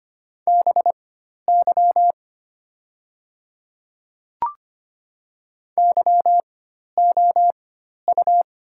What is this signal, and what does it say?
Morse code, heard as a keyed tone, about 25 wpm, spaced as for 12 wpm, 700 Hz. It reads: BY YOU
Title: 4x Character Spacing - Top 200 Words - 25wpm